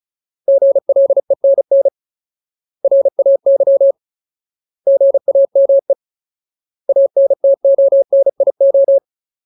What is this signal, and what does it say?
Morse code, heard as a keyed tone, about 34 wpm, 550 Hz. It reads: GLENN RAY GAME ANTONIO
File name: Ultimate Word List / 35WPM Ultimate